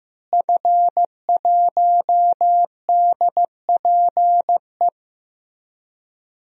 Morse code, heard as a keyed tone, about 15 wpm, 700 Hz. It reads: F1DPE
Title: DXCC-100 Call Signs - 5 Characters - Repeated 3x - 15wpm